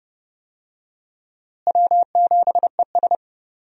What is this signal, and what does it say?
Morse code, heard as a keyed tone, about 30 wpm, 700 Hz. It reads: W7ES
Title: USA Call Signs - 4 Characters - Repeated 2x - 30wpm